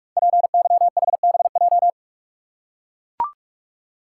Morse code, heard as a keyed tone, about 44 wpm, 700 Hz. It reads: PYHBJ
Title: Code Groups - 5 Letters - 45wpm